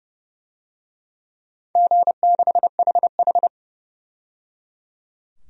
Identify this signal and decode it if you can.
Morse code, heard as a keyed tone, about 30 wpm, 700 Hz. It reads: G6HH
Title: DXCC-100 Call Signs - 4 Characters - 30wpm